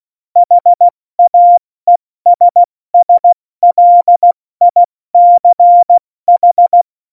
Morse code, heard as a keyed tone, about 16 wpm, 700 Hz. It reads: HAESSLICH
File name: Morsekurs / Witze / Witz 23. Oktober 2018, 70 F80